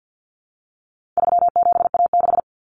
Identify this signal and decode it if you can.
Morse code, heard as a keyed tone, about 48 wpm, 700 Hz. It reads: 37A6
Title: Common Cut Numbers - 4-Digit 50wpm